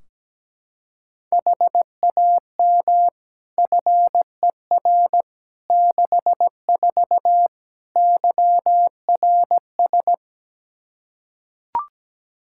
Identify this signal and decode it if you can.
Morse code, heard as a keyed tone, about 17 wpm, 700 Hz. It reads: HAM FER 64 YRS
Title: Practice QSO - Exchange 3 - More Info - 17wpm